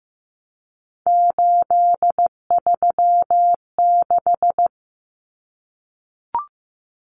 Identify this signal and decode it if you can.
Morse code, heard as a keyed tone, about 15 wpm, 700 Hz. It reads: 836
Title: Common Cut Numbers - 3-Digit 15wpm